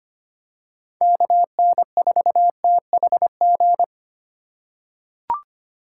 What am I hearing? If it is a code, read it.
Morse code, heard as a keyed tone, about 25 wpm, 700 Hz. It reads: KN4THG
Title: USA Call Signs - 6 Characters - 4x Character Spacing - 25wpm